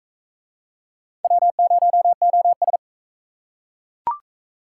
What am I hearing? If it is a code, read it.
Morse code, heard as a keyed tone, about 35 wpm, 700 Hz. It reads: W0OS
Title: Speed Racing - USA 1x2 Call Signs 28wpm